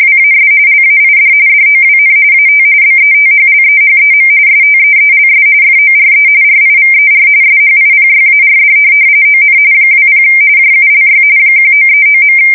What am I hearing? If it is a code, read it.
RTTY, about 45.45 baud. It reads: WAS LAID DOWN AT BETHLEHEM STEELS FORE RIVER SHIPYARD ON 20 JULY 1939.
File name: fsk170 / USS-Massachusetts